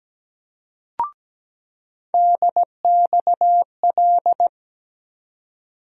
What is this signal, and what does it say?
Morse code, heard as a keyed tone, about 17 wpm, 700 Hz. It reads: DXL
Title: USA Call Sign Postfixes - 3 Letters 17wpm